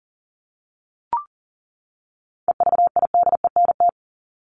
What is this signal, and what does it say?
Morse code, heard as a keyed tone, about 40 wpm, 700 Hz. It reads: EVIDENT
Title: LY Suffix - Top 100 Words with Base Words 40wpm